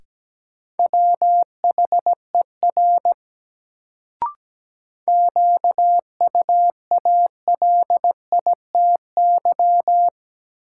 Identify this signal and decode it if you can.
Morse code, heard as a keyed tone, about 17 wpm, 700 Hz. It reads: WHER QUALITY
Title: Sentences from Top 1500 Words - New Word Drill Alt - 17wpm